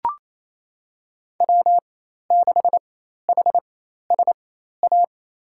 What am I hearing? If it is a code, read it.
Morse code, heard as a keyed tone, about 28 wpm, spaced as for 13 wpm, 700 Hz. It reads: W6HSA
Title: USA Call Signs - 5 Characters - 4x Character Spacing - 28wpm